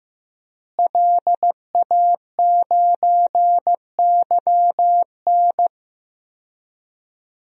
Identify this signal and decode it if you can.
Morse code, heard as a keyed tone, about 15 wpm, 700 Hz. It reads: LA9YN
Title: DXCC-100 Call Signs - 5 Characters - Repeated 2x - 15wpm